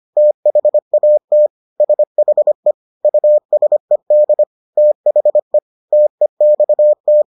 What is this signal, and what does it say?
Morse code, heard as a keyed tone, about 25 wpm, 600 Hz. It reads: THAT SHE USED THE TEXT